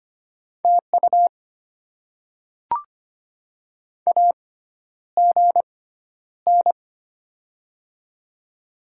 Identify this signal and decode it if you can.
Morse code, heard as a keyed tone, about 25 wpm, spaced as for 11 wpm, 700 Hz. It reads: TU AGN
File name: Top 20 QSO Elements - 6x Character Spacing - 25wpm